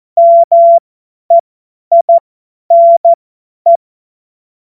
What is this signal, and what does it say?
Morse code, heard as a keyed tone, about 14 wpm, spaced as for 10 wpm, 680 Hz. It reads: MEINE
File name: Morsekurs / Witze / Witz 24. Januar 2023 50 F70